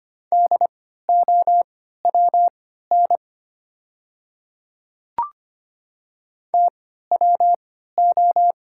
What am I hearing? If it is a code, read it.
Morse code, heard as a keyed tone, about 25 wpm, spaced as for 14 wpm, 690 Hz. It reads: DOWN TWO
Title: 3x Character Spacing - Top 100 Words - 25wpm